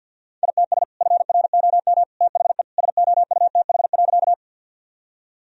Morse code, heard as a keyed tone, about 50 wpm, 700 Hz. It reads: ITS FROM THE SOUTH.
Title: Sentences from Top 1000 Words - Review Part 2 - 50wpm